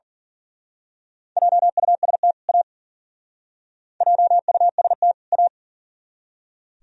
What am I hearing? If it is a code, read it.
Morse code, heard as a keyed tone, about 44 wpm, 700 Hz. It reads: JUST A JUST A
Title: Speed-Racing - Sets of 2 Words 40wpm